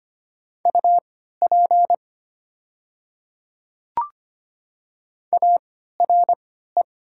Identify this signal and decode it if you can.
Morse code, heard as a keyed tone, about 25 wpm, spaced as for 14 wpm, 700 Hz. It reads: UP ARE
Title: 3x Character Spacing - Top 100 Words - 25wpm